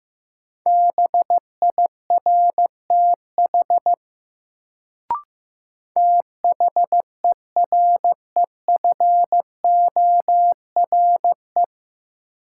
Morse code, heard as a keyed tone, about 15 wpm, 700 Hz. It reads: BIRTH THEREFORE
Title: Sentences from Top 800 Words - New Word Drill - 15wpm